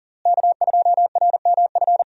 Morse code, heard as a keyed tone, about 40 wpm, 700 Hz. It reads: K2RMF
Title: USA Call Signs - 5 Characters - Repeated 2x - 40wpm